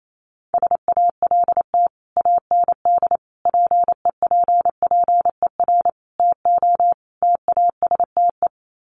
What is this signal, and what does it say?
Morse code, heard as a keyed tone, about 28 wpm, 700 Hz. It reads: SALT AND PEPPER TO TASTE
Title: Sets of 5 Words 28wpm